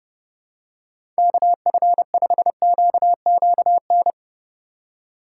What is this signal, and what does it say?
Morse code, heard as a keyed tone, about 30 wpm, 700 Hz. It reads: KF5QQN
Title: USA Call Signs - 6 Characters - 4x Character Spacing - 30wpm